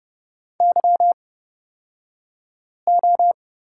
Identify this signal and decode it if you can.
Morse code, heard as a keyed tone, about 30 wpm, spaced as for 5 wpm, 700 Hz. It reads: YO
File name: Lesson 37d - Letter x - Two Characters 30wpm